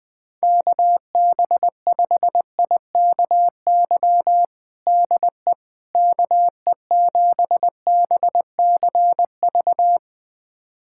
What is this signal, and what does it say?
Morse code, heard as a keyed tone, about 21 wpm, 700 Hz. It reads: KB5IKY DE KE7BCV
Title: USA Callsign DE Callsign - 6 Characters - 20wpm